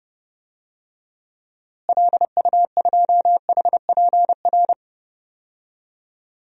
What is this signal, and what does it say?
Morse code, heard as a keyed tone, about 32 wpm, 700 Hz. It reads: LU2HPR
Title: DXCC-100 Call Signs - 6 Characters - Repeated 2x - 30wpm